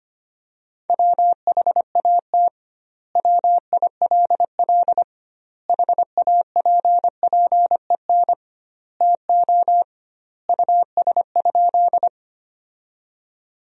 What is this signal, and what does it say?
Morse code, heard as a keyed tone, about 25 wpm, 700 Hz. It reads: WHAT WILL HAPPEN TO US?
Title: 2x Word-Spacing - Sentences from Top 400 Words - 25wpm